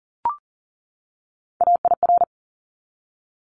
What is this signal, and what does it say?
Morse code, heard as a keyed tone, about 40 wpm, 700 Hz. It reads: AIR
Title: Top 200 Words 40wpm